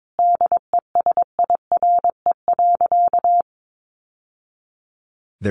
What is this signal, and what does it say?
Morse code, heard as a keyed tone, about 22 wpm, 700 Hz. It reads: DESIRE.
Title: Sentences from Top 1100 Words 22wpm